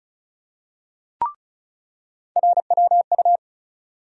Morse code, heard as a keyed tone, about 36 wpm, 700 Hz. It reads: RWU
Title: USA Call Sign Postfixes - 3 Letters 35wpm